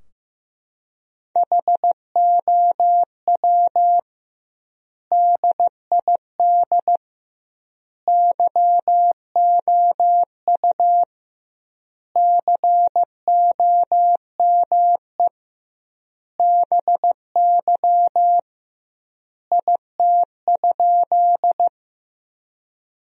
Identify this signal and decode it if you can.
Morse code, heard as a keyed tone, about 15 wpm, 700 Hz. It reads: HOW DID YOU COME BY IT?
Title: 2x Word-Spacing - Sentences from Top 100 Words - 15wpm